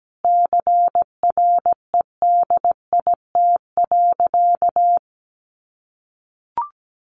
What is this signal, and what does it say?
Morse code, heard as a keyed tone, about 17 wpm, 700 Hz. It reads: CREDIT.
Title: Sentences from Top 1000 Words 17wpm